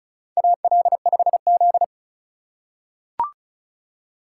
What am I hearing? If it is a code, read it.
Morse code, heard as a keyed tone, about 36 wpm, 700 Hz. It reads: AL5Z
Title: USA Call Signs - 4 Characters - 3x Character Spacing - 35wpm